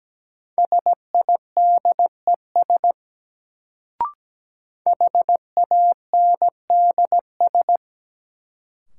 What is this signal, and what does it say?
Morse code, heard as a keyed tone, about 17 wpm, 700 Hz. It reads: SIDES HANDS
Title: Sets of 4 Words - New Word Drill - 17wpm